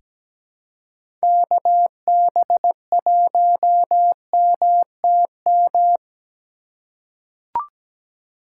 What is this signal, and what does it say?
Morse code, heard as a keyed tone, about 17 wpm, 700 Hz. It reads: KB1MTM
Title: USA Call Signs - 3x Character Spacing - 17wpm